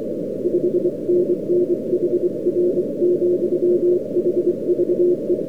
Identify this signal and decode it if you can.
Morse code, heard as a keyed tone, about 23 wpm, 360 Hz, with noise. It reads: 5NN HR, HVE